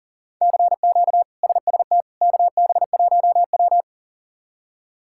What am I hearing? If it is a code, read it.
Morse code, heard as a keyed tone, about 37 wpm, 700 Hz. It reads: CQ SST KB1W
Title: K1USN SST QSO Practice - 40wpm